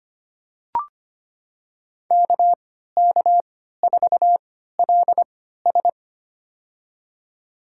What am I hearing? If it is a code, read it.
Morse code, heard as a keyed tone, about 25 wpm, spaced as for 14 wpm, 700 Hz. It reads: KK4LS
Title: USA Call Signs - 3x Character Spacing - 25wpm